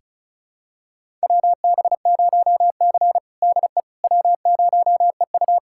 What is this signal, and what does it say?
Morse code, heard as a keyed tone, about 34 wpm, 700 Hz. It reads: WB0C DE W0EU
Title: USA Callsign DE Callsign - 4 Characters - 35wpm